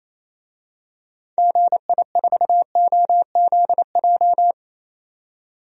Morse code, heard as a keyed tone, about 27 wpm, 700 Hz. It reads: GI4OZJ
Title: DXCC-100 Call Signs - 6 Characters - 28wpm